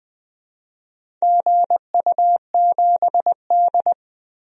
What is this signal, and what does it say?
Morse code, heard as a keyed tone, about 20 wpm, 700 Hz. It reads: GU7D
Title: DXCC-100 Call Signs - 4 Characters - 20wpm